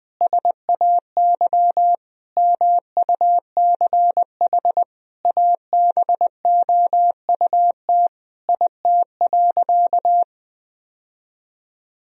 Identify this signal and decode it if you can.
Morse code, heard as a keyed tone, about 21 wpm, 700 Hz. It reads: SAY MUCH ABOUT IT.
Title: Sentences from Top 1000 Words - Review Part 2 - 20wpm